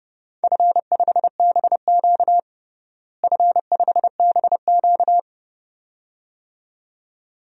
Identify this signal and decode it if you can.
Morse code, heard as a keyed tone, about 30 wpm, 700 Hz. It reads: F5BQ F5BQ
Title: DXCC-100 Call Signs - 4 Characters - Repeated 2x - 30wpm